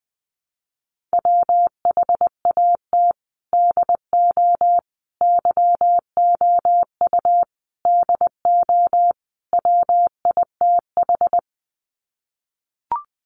Morse code, heard as a keyed tone, about 20 wpm, 700 Hz. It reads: WHAT DO YOU DO WITH
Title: Sets of 5 Words Encore2 20wpm